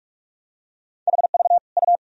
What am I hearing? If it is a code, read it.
Morse code, heard as a keyed tone, about 46 wpm, 700 Hz. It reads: HV U
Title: Sentences from Top 400 Words and Abbreviations 45wpm